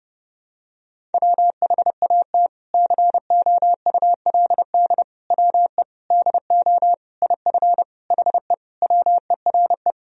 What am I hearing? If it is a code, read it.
Morse code, heard as a keyed tone, about 30 wpm, 700 Hz. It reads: WHAT COULD WE DO IF HE WERE
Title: Sentences from Top 400 Words and Abbreviations 30wpm